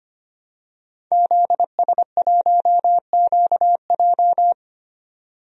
Morse code, heard as a keyed tone, about 25 wpm, 690 Hz. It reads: ZS1QJ
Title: DXCC-100 Call Signs - 5 Characters - 25wpm